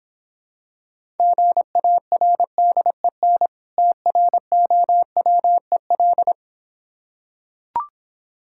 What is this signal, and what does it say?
Morse code, heard as a keyed tone, about 26 wpm, 700 Hz. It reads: GARDEN TROWEL